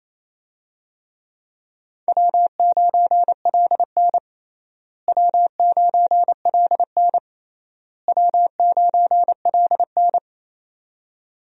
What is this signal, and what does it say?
Morse code, heard as a keyed tone, about 28 wpm, 700 Hz. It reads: W9LN W9LN W9LN
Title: USA Call Signs - 4 Characters - Repeated 3x - 28wpm